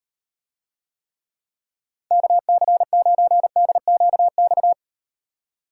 Morse code, heard as a keyed tone, about 38 wpm, 700 Hz. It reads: KC9DQX